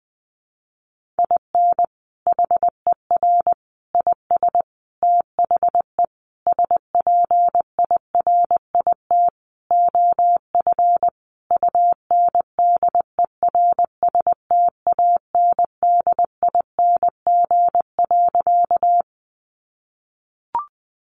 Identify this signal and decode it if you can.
Morse code, heard as a keyed tone, about 20 wpm, 700 Hz. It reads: IN HER IS THE SPIRIT OF UNDERSTANDING.
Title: Sentences from Top 1400 Words 20wpm